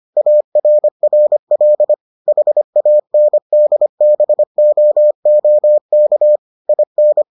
Morse code, heard as a keyed tone, about 25 wpm, 600 Hz. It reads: ARRL HANDBOOK IN